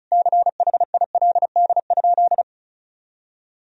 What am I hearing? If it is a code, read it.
Morse code, heard as a keyed tone, about 36 wpm, 700 Hz. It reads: CHILD?
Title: Sentences from Top 1000 Words - Review Part 1 - 35wpm